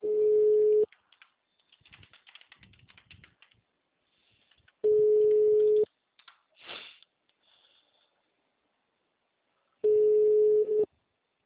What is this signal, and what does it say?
Ringback tone (Europe, 425 Hz).